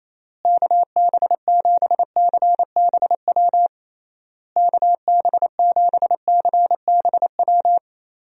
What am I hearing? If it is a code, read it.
Morse code, heard as a keyed tone, about 28 wpm, 700 Hz. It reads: KB7CBW KB7CBW
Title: USA Call Signs - 6 Characters - Repeated 3x - 28wpm